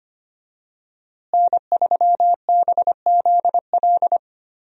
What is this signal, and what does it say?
Morse code, heard as a keyed tone, about 25 wpm, 700 Hz. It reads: N3BZL
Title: USA Call Signs - 5 Characters - Repeated 3x - 25wpm